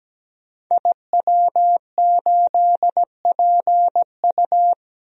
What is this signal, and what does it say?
Morse code, heard as a keyed tone, about 17 wpm, 700 Hz. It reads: IW8PU
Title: DXCC-100 Call Signs - 5 Characters - Repeated 2x - 17wpm